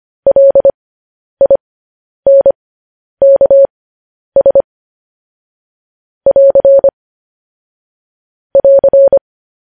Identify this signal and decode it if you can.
Morse code, heard as a keyed tone, about 25 wpm, spaced as for 10 wpm, 550 Hz. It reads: LINKS <AR> <AR>